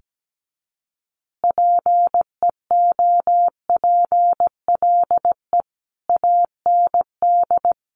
Morse code, heard as a keyed tone, about 17 wpm, 700 Hz. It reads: PEOPLE AND
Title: Easy Phrases 17wpm